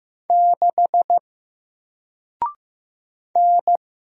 Morse code, heard as a keyed tone, about 15 wpm, 700 Hz. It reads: 6 N
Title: Single Character (Letter:Number:Punctuation:Prosign) 15wpm